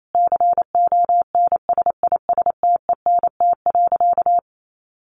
Morse code, heard as a keyed tone, about 28 wpm, 700 Hz. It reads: CONSISTENT.
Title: Sentences from Top 1500 Words 28wpm